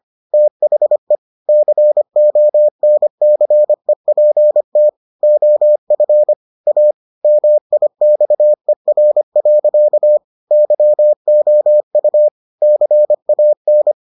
Morse code, heard as a keyed tone, about 25 wpm, 600 Hz. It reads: THE CONCEPT OF A MIXER. YOU CAN